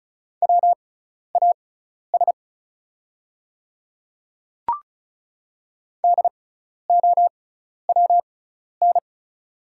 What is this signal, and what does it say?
Morse code, heard as a keyed tone, about 36 wpm, spaced as for 12 wpm, 700 Hz. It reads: WAS DOWN
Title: 6x Character Spacing - Top 100 Words - 35wpm